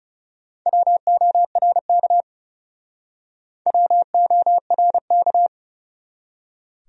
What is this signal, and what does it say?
Morse code, heard as a keyed tone, about 38 wpm, 700 Hz. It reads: WORK WORK